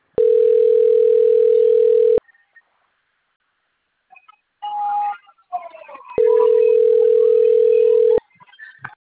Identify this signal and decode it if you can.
Ringback tone (North America), 440 + 480 Hz.